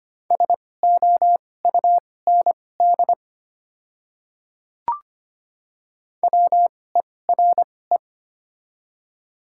Morse code, heard as a keyed tone, about 25 wpm, spaced as for 18 wpm, 700 Hz. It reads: SOUND WERE
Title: 2x Character Spacing - Top 100 Words - 25wpm